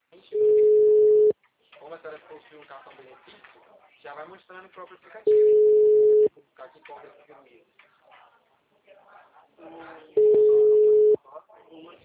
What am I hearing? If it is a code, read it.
Ringback tone (Europe, 425 Hz), with speech.
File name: out- -2004- - - .2